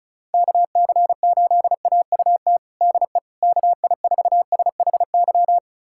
Morse code, heard as a keyed tone, about 36 wpm, 700 Hz. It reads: KC8AUT DE KI4SHY
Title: USA Callsign DE Callsign - 6 Characters - 35wpm